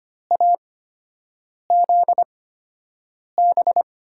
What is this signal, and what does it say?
Morse code, heard as a keyed tone, about 25 wpm, spaced as for 7 wpm, 700 Hz. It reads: AZB
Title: USA Call Signs - 6 Characters - 8x Character Spacing - 25wpm